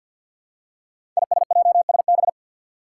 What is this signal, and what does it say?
Morse code, heard as a keyed tone, about 47 wpm, 700 Hz. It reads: ISJSB
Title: Five Challenging Letter-Number Combinations 50wpm